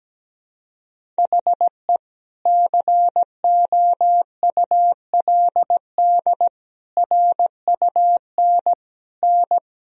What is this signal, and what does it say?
Morse code, heard as a keyed tone, about 17 wpm, 700 Hz. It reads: HE COULD RUN N
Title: Sentences from Top 300 Words and Abbreviations 17wpm